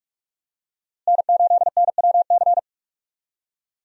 Morse code, heard as a keyed tone, about 46 wpm, 700 Hz. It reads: N8NWC
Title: USA Call Signs - 5 Characters - Repeated 3x - 45wpm